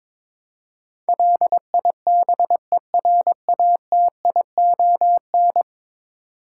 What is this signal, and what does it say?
Morse code, heard as a keyed tone, about 22 wpm, 700 Hz. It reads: LIBERATION
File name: practice / 22wpm / 10 Letter Words - First 3 Letters Given - 22wpm